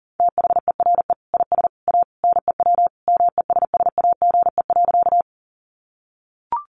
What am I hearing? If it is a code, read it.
Morse code, heard as a keyed tone, about 40 wpm, 700 Hz. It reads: THERE IS A NEW MESSAGE.
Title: Sentences from Top 1100 Words 40wpm